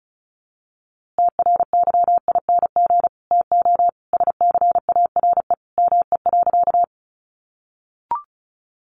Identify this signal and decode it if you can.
Morse code, heard as a keyed tone, about 34 wpm, 700 Hz. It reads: TRYING TO SCARE ME.
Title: A Sound Of Thunder 35wpm